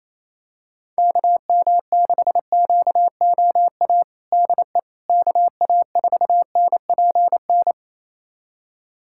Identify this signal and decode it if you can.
Morse code, heard as a keyed tone, about 28 wpm, 700 Hz. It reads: KM6QOA DE KA4NPN
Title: USA Callsign DE Callsign - 6 Characters - 28wpm